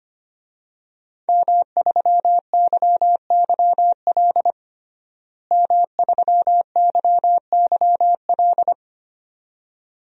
Morse code, heard as a keyed tone, about 25 wpm, 700 Hz. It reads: M3YYL M3YYL
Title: DXCC-100 Call Signs - 5 Characters - Repeated 2x - 25wpm